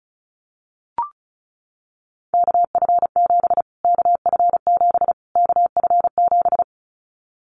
Morse code, heard as a keyed tone, about 34 wpm, 700 Hz. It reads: KF7 KF7 KF7
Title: Callsign Prefixes for Top 25 DXCC - Repeated 3x - 35wpm